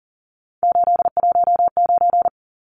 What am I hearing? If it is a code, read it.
Morse code, heard as a keyed tone, about 40 wpm, 700 Hz. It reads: 819
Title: Numbers - 3-Digit 40wpm